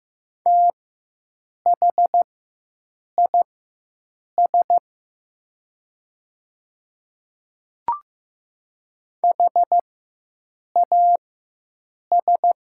Morse code, heard as a keyed tone, about 15 wpm, spaced as for 7 wpm, 700 Hz. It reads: THIS HAS
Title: 4x Character Spacing - Top 100 Words - 15wpm